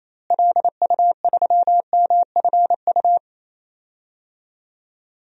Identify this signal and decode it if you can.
Morse code, heard as a keyed tone, about 28 wpm, 700 Hz. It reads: LU3MFU
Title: DXCC-100 Call Signs - 6 Characters - Repeated 2x - 28wpm